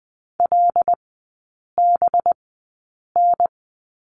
Morse code, heard as a keyed tone, about 20 wpm, spaced as for 8 wpm, 700 Hz. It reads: LBN
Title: 184 Lesson 30e - Letter b - Three Characters 20wpm